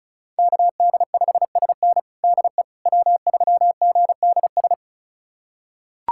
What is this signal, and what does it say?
Morse code, heard as a keyed tone, about 36 wpm, 700 Hz. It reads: KD5SN DE W3GDS
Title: USA Callsign DE Callsign 35wpm